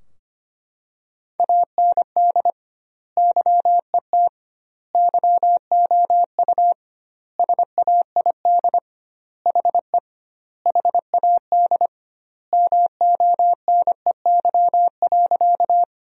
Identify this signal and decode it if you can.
Morse code, heard as a keyed tone, about 25 wpm, 700 Hz. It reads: AND YET YOU SAID HE HAD MONEY.
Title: 2x Word-Spacing - Sentences from Top 500 Words - 25wpm